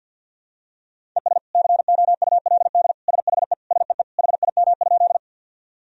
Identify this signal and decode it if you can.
Morse code, heard as a keyed tone, about 50 wpm, 700 Hz. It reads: ES COULD SHE SEE HIM?
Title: Sentences from Top 100 Words and Abbreviations 50wpm